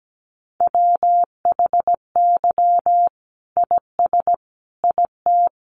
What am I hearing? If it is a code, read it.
Morse code, heard as a keyed tone, about 17 wpm, 700 Hz. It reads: WHY IS IT
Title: Sentences from Top 900 Words 17wpm